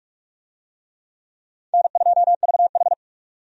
Morse code, heard as a keyed tone, about 46 wpm, 700 Hz. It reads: N2VH
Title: USA Call Signs - 4 Characters - 45wpm